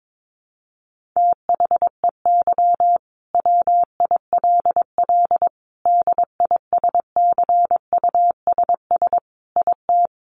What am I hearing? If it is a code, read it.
Morse code, heard as a keyed tone, about 22 wpm, 700 Hz. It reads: THEY WILL DISCUSS IT
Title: Sentences from Top 1400 Words 22wpm